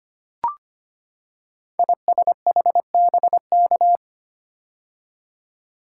Morse code, heard as a keyed tone, about 25 wpm, 700 Hz. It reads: ISHBK
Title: Code Groups - 5 Letters - 25wpm